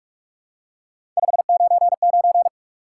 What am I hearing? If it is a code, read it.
Morse code, heard as a keyed tone, about 43 wpm, 700 Hz. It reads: 599
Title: Top 180 QSO Elements 45wpm